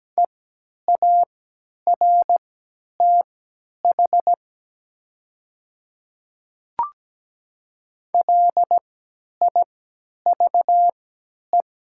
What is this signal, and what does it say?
Morse code, heard as a keyed tone, about 17 wpm, spaced as for 10 wpm, 700 Hz. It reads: EARTH LIVE
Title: 3x Character Spacing - Top 200 Words - 17wpm